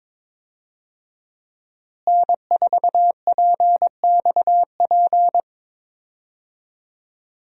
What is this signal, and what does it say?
Morse code, heard as a keyed tone, about 22 wpm, 700 Hz. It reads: N4PXP
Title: USA Call Signs - 5 Characters - Repeated 3x - 22wpm